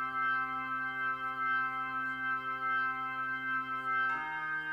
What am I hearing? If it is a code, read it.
Emergency Alert System (SAME) tones, heard as harsh buzzing bursts. It is heard from 0.7 to 3.7 s.